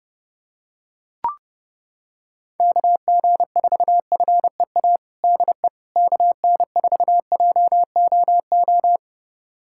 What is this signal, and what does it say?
Morse code, heard as a keyed tone, about 30 wpm, 700 Hz. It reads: KG4FEA DE KN4JOO
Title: USA Callsign DE Callsign - 6 Characters - 30wpm